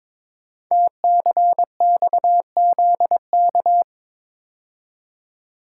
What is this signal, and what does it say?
Morse code, heard as a keyed tone, about 22 wpm, 700 Hz. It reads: TCXZK
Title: Code Groups - 5 Letters - 22wpm